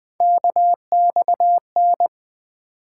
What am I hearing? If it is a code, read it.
Morse code, heard as a keyed tone, about 20 wpm, 700 Hz. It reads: KXN
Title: Three Letter-Number Combinations - 20wpm